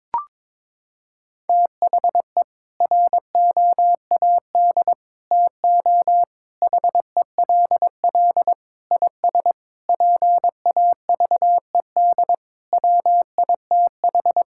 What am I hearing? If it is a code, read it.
Morse code, heard as a keyed tone, about 22 wpm, 700 Hz. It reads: THE ROAD TO HELL IS PAVED WITH
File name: practice / 22wpm / Sentences with 8-Word Phrases - 22wpm